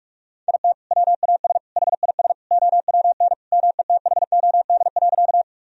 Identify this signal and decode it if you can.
Morse code, heard as a keyed tone, about 46 wpm, 700 Hz. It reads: IT WAS HIS OWN METHOD.
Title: Sentences from Top 1000 Words - Review Part 3 - 45wpm